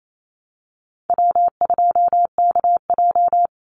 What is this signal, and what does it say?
Morse code, heard as a keyed tone, about 28 wpm, 700 Hz. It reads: W2KJ
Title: Call Signs - 28wpm